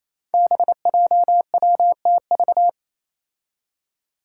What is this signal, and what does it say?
Morse code, heard as a keyed tone, about 28 wpm, 700 Hz. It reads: BJWTV
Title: Five Challenging Letter-Number Combinations 28wpm